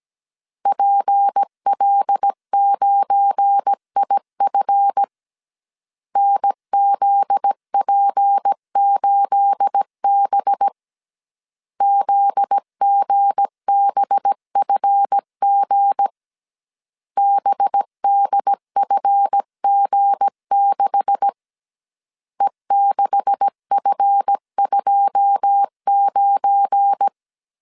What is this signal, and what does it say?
Morse code, heard as a keyed tone, about 17 wpm, 800 Hz. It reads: PL9IF NZP8B ZGBFG BDFG6 E6F29
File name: CW / DET / 1 9 1500C S A 16wpm 2828